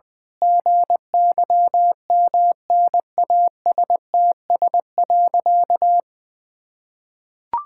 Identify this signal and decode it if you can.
Morse code, heard as a keyed tone, about 20 wpm, 700 Hz. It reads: GYMNASTS.
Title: Pangram Challenge v2 - 20wpm